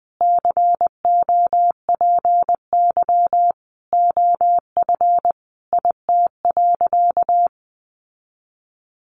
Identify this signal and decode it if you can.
Morse code, heard as a keyed tone, about 20 wpm, 700 Hz. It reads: COPY OF IT.
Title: Sentences from Top 1400 Words 20wpm